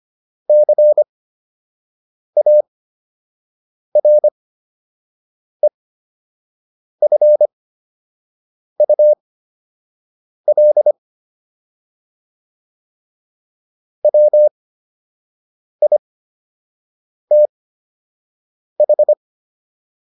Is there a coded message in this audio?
Yes: Morse code from 0.5 to 19.1 s, heard as a keyed tone, about 25 wpm, spaced as for 6 wpm, 600 Hz. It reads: CAREFUL WITH